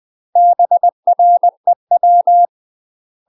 Morse code, heard as a keyed tone, about 20 wpm, 700 Hz. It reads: BREW
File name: UK MASTER QSOs TRAINING / 20WPM / UK MASTER QSOs 0015